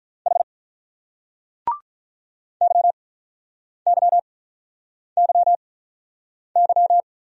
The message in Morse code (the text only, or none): H YYYY